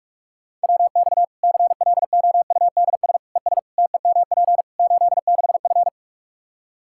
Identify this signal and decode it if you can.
Morse code, heard as a keyed tone, about 42 wpm, 700 Hz. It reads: WX CLOUDS ES TEMP 86F
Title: Practice QSO - Exchange 2 - Info - 45wpm